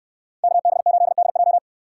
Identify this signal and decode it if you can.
Morse code, heard as a keyed tone, about 69 wpm, 700 Hz. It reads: KB8DJ